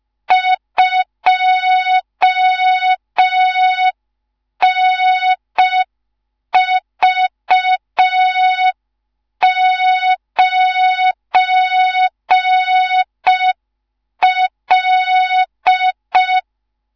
Morse code, heard as a keyed tone, about 5 wpm, 3700 Hz. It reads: 2NV9L
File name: Morze-HA2MN / hang / 19 GYAKORLAT